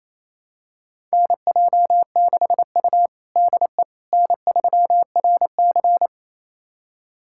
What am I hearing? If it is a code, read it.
Morse code, heard as a keyed tone, about 28 wpm, 700 Hz. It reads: NJ6U DE N3RC
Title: USA Callsign DE Callsign - 4 Characters - 28wpm